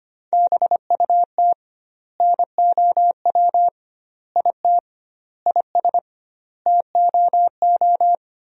Morse code, heard as a keyed tone, about 25 wpm, 700 Hz. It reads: BUT NOW IT IS TOO
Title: 2x Word-Spacing - Sentences from Top 300 Words - 25wpm